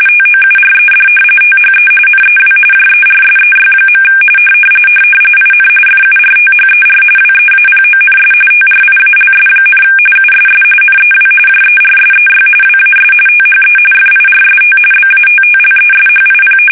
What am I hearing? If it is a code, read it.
RTTY, about 45.45 baud. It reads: TOURED THE NEW JERSEY. IN FURTHER WEAPONS TRIALS, USS NEW JERSEY WAS THE FIRST BATTLESHIP
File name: fsk850 / battleshipnj